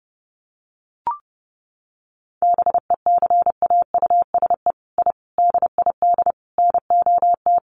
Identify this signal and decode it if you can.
Morse code, heard as a keyed tone, about 30 wpm, 700 Hz. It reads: BECAUSE I DID NOT
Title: Sets of 5 Words Encore2 30wpm